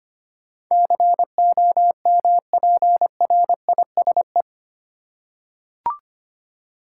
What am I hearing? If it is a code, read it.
Morse code, heard as a keyed tone, about 25 wpm, 700 Hz. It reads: COMPRISE
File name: 8 Letter Words - First 3 Letters Given - 25wpm